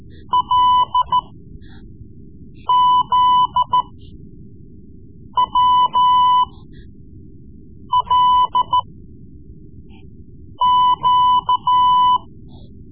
Morse code, heard as a keyed tone, about 13 wpm, spaced as for 5 wpm, 1000 Hz. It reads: LZWLQ
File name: Morse Lessons / Morse Lessons / Tape-3B